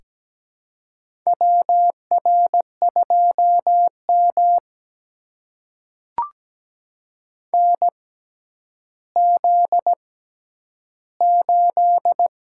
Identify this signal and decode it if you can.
Morse code, heard as a keyed tone, about 17 wpm, spaced as for 10 wpm, 700 Hz. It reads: WR2M NZ8
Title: USA Call Signs - 4 Characters - 6x Character Spacing - 17wpm